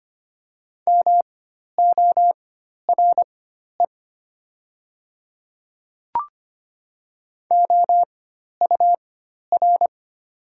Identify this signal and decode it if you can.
Morse code, heard as a keyed tone, about 25 wpm, spaced as for 12 wpm, 700 Hz. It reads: MORE OUR
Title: 4x Character Spacing - Top 200 Words - 25wpm